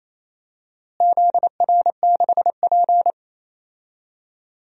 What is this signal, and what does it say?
Morse code, heard as a keyed tone, about 28 wpm, 700 Hz. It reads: ZR6P
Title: DXCC-100 Call Signs - 4 Characters - Repeated 3x - 28wpm